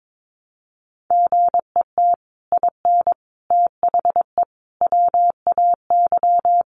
Morse code, heard as a keyed tone, about 22 wpm, 700 Hz. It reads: GET IN THE WAY
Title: Sets of 5 Words Encore 22wpm